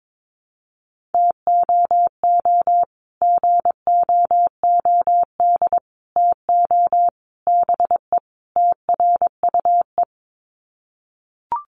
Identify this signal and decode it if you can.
Morse code, heard as a keyed tone, about 22 wpm, 700 Hz. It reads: TOO GOOD TO BE TRUE
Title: Sets of 5 Words Encore 22wpm